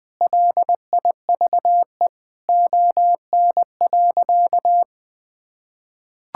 Morse code, heard as a keyed tone, about 20 wpm, 700 Hz. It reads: LIVE ON.
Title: Sentences from Top 1000 Words - Review Part 6 - 20wpm